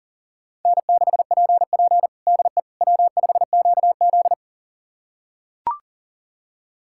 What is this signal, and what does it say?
Morse code, heard as a keyed tone, about 40 wpm, 700 Hz. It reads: N6PP DE W5QZ
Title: USA Callsign DE Callsign - 4 Characters - 40wpm